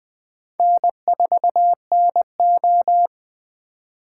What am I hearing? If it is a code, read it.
Morse code, heard as a keyed tone, about 22 wpm, 700 Hz. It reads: N4NO